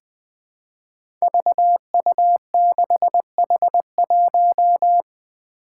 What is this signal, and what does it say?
Morse code, heard as a keyed tone, about 20 wpm, 700 Hz. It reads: VU6H1
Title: Five Challenging Letter-Number Combinations 20wpm